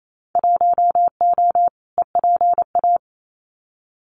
Morse code, heard as a keyed tone, about 28 wpm, 700 Hz. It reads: 1O EPA
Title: Winter Field Day Exchanges 28wpm